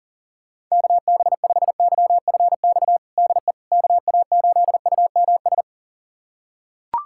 Morse code, heard as a keyed tone, about 40 wpm, 700 Hz. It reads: KB5YFX DE KA8UMS
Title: USA Callsign DE Callsign - 6 Characters - 40wpm